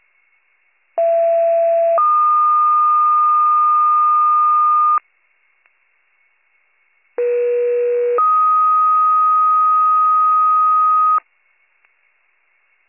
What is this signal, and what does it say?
Two-tone page, heard as a one-second tone, then a three-second tone, about 670 then 1220 Hz, about 480 then 1250 Hz. It is heard from 1.0 to 11.2 s.